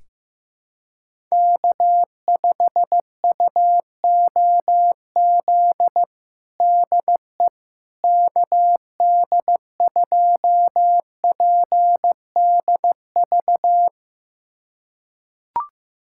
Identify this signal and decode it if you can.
Morse code, heard as a keyed tone, about 15 wpm, 700 Hz. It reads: K5UOZ DE KD2PDV
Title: USA Callsign DE Callsign 15wpm